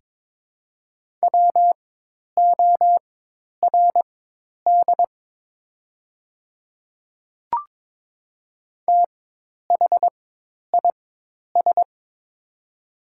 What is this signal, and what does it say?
Morse code, heard as a keyed tone, about 23 wpm, spaced as for 10 wpm, 700 Hz. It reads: WORD THIS
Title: 4x Character Spacing - Top 100 Words - 22wpm